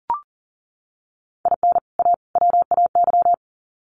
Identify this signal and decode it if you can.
Morse code, heard as a keyed tone, about 40 wpm, 700 Hz. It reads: IN A WAY
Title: Sets of 3 Words 40wpm